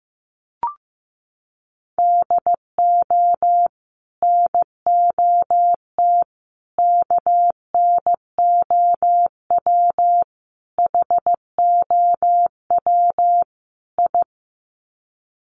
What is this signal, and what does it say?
Morse code, heard as a keyed tone, about 15 wpm, 700 Hz. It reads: DO NOT KNOW HOW I
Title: Sets of 5 Words Encore2 15wpm